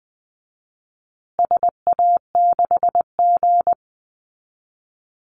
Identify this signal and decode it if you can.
Morse code, heard as a keyed tone, about 20 wpm, 700 Hz. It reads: SA6G
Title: Call Signs Repeated 3x - 20wpm